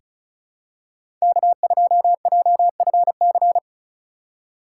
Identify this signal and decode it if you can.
Morse code, heard as a keyed tone, about 36 wpm, 700 Hz. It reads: K2JFC
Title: USA Call Signs - 5 Characters - 4x Character Spacing - 35wpm